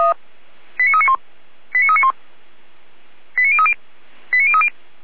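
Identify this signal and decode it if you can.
Five-tone selective call (ZVEI), heard as quick runs of five tones, tones of about 70 ms. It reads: 89391 89391 89030 89030
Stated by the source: ZVEI five-tone call 89508 received by SDR, from the feuerwehr-orlamuende.org server (fire brigade Orlamünde).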